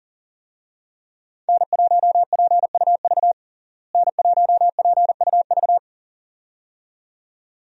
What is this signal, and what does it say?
Morse code, heard as a keyed tone, about 37 wpm, 700 Hz. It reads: N1PUV N1PUV
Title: USA Call Signs - 5 Characters - Repeated 2x - 40wpm